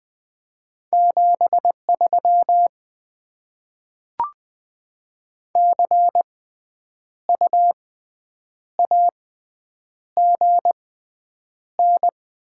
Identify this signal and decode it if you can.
Morse code, heard as a keyed tone, about 20 wpm, spaced as for 8 wpm, 690 Hz. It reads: 73 CU AGN
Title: Top 20 QSO Elements - 6x Character Spacing - 20wpm